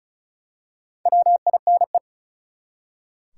Morse code, heard as a keyed tone, about 36 wpm, 700 Hz. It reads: WINE
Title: Four Letter Words - 35wpm